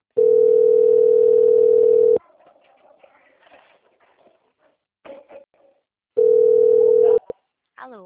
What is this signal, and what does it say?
Ringback tone (North America), 440 + 480 Hz, with speech.
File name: out- - - - - .7425